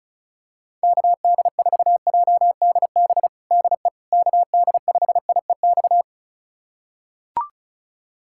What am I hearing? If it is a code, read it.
Morse code, heard as a keyed tone, about 34 wpm, 700 Hz. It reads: KD4JDB DE KD5IEX
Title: USA Callsign DE Callsign 35wpm